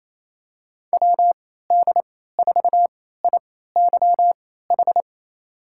Morse code, heard as a keyed tone, about 27 wpm, spaced as for 16 wpm, 690 Hz. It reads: WD4IYH
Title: USA Call Signs - 3x Character Spacing - 28wpm